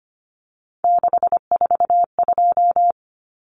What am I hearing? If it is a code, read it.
Morse code, heard as a keyed tone, about 25 wpm, 700 Hz. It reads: 642
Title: Numbers - 3-Digit 25wpm